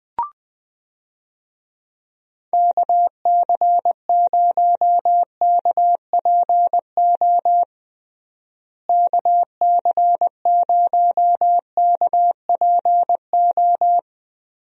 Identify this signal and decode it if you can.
Morse code, heard as a keyed tone, about 21 wpm, 700 Hz. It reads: KC0KPO KC0KPO
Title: USA Call Signs - 6 Characters - Repeated 3x - 20wpm